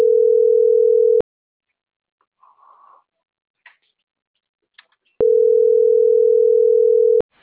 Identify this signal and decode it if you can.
Ringback tone (North America), 440 + 480 Hz.
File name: CENTRAL MABE / out edwinomarojeda - .18217